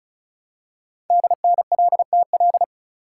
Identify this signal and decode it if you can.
Morse code, heard as a keyed tone, about 36 wpm, 700 Hz. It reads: DNLTL
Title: Code Groups - 5 Letters - 35wpm